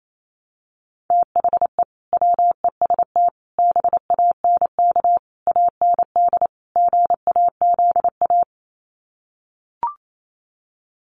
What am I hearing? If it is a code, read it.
Morse code, heard as a keyed tone, about 28 wpm, 700 Hz. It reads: THE WEST BANK AND GAZA
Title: Sets of 5 Words 28wpm